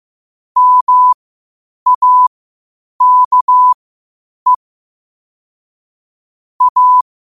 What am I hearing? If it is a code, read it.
Morse code, heard as a keyed tone, about 15 wpm, spaced as for 8 wpm, 1000 Hz. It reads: MAKE A
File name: qotdfiles / 8wpm